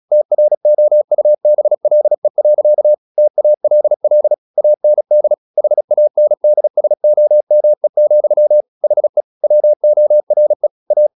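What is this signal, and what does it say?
Morse code, heard as a keyed tone, about 36 wpm, 600 Hz. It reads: TROUBLE. TALL AND HANDSOME, HE WORE A
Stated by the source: Morse practice at 36 wpm, text of Ken Follett's novel Never, from CW med Gnister.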